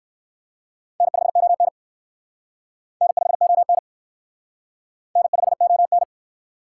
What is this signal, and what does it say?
Morse code, heard as a keyed tone, about 56 wpm, 700 Hz. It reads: N5ON N5ON N5ON
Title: Speed Racing - USA Call Signs 45wpm